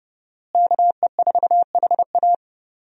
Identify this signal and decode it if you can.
Morse code, heard as a keyed tone, about 32 wpm, 700 Hz. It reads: KE4HA
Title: USA Call Signs - 5 Characters - Repeated 3x - 30wpm